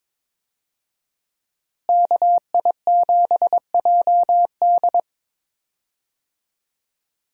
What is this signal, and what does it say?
Morse code, heard as a keyed tone, about 22 wpm, 700 Hz. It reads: KI7JD
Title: USA Call Signs - 5 Characters - Repeated 2x - 22wpm